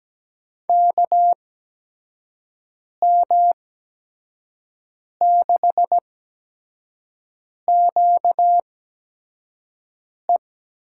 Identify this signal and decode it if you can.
Morse code, heard as a keyed tone, about 17 wpm, spaced as for 5 wpm, 700 Hz. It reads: KM6QE
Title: USA Call Signs - 6 Characters - 8x Character Spacing - 17wpm